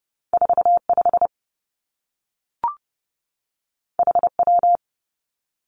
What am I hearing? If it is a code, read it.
Morse code, heard as a keyed tone, about 30 wpm, 700 Hz. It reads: 4 5 H W
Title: Two Challenging Letter-Number Combinations 30wpm